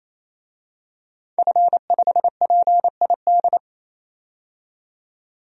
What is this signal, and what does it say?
Morse code, heard as a keyed tone, about 27 wpm, 700 Hz. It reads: F5PID
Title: DXCC-100 Call Signs - 5 Characters - 28wpm